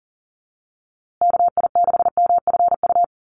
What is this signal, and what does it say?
Morse code, heard as a keyed tone, about 40 wpm, 700 Hz. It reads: KI6MFU
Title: USA Call Signs Repeated 3x 40wpm